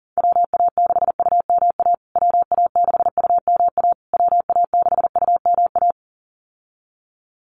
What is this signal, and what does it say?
Morse code, heard as a keyed tone, about 40 wpm, 700 Hz. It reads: WA6UMA WA6UMA WA6UMA
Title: USA Call Signs Repeated 3x 40wpm